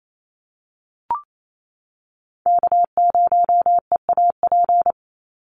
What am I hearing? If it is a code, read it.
Morse code, heard as a keyed tone, about 28 wpm, 700 Hz. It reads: K0EAP
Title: NA SOTA Chaser Callsigns 28wpm